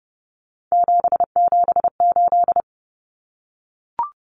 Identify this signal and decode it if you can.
Morse code, heard as a keyed tone, about 30 wpm, 700 Hz. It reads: 778
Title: Numbers - 3-Digit 30wpm